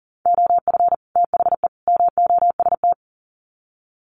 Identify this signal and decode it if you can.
Morse code, heard as a keyed tone, about 40 wpm, 700 Hz. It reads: OF THE MOST
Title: Sets of 3 Words 40wpm